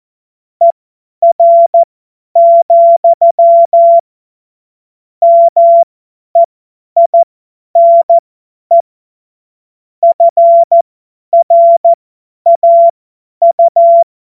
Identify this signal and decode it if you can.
Morse code, heard as a keyed tone, about 14 wpm, spaced as for 10 wpm, 680 Hz. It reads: ER, MEINE FRAU
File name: Morsekurs / Witze / Witz 19 Dezember 2023 50 F70